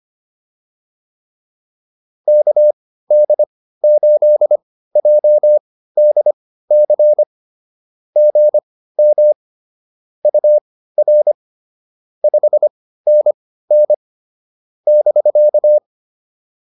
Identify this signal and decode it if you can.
Morse code, heard as a keyed tone, about 25 wpm, spaced as for 15 wpm, 600 Hz. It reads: KD8JDC GM UR 5NN <BK>